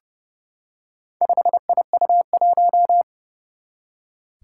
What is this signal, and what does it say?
Morse code, heard as a keyed tone, about 29 wpm, 700 Hz. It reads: 5IU1
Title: Four Challenging Letter-Number Combinations 30wpm